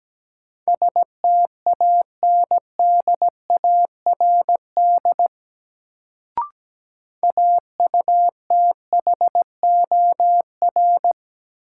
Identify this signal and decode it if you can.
Morse code, heard as a keyed tone, about 17 wpm, 700 Hz. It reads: STANDARD AUTHOR
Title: Sentences from Top 1200 Words - New Word Drill Alt - 17wpm